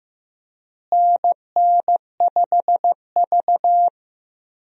Morse code, heard as a keyed tone, about 15 wpm, 700 Hz. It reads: NN5V
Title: USA Call Signs - 4 Characters - Repeated 3x - 15wpm